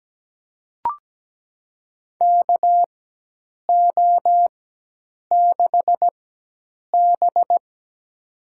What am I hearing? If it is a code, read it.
Morse code, heard as a keyed tone, about 17 wpm, spaced as for 8 wpm, 700 Hz. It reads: KO6B
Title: USA Call Signs - 4x Character Spacing - 17wpm